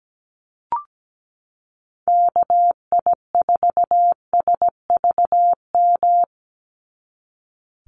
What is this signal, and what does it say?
Morse code, heard as a keyed tone, about 17 wpm, 700 Hz. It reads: KI4SVM
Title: NA SOTA Activator Callsigns 17wpm